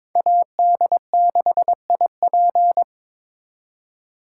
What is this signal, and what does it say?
Morse code, heard as a keyed tone, about 26 wpm, 700 Hz. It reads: AD6IP
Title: Speed Racing - USA 2x2 Call Signs 22wpm